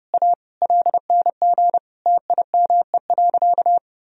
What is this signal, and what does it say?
Morse code, heard as a keyed tone, about 30 wpm, 700 Hz. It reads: A LNG TIME.
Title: Sentences from Top 400 Words and Abbreviations 30wpm